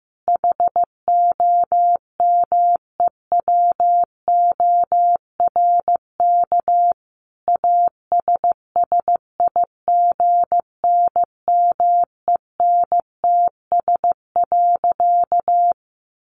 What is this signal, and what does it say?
Morse code, heard as a keyed tone, about 15 wpm, 700 Hz. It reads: HOMEWORK ASSIGNMENTS.
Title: Easy Phrases 15wpm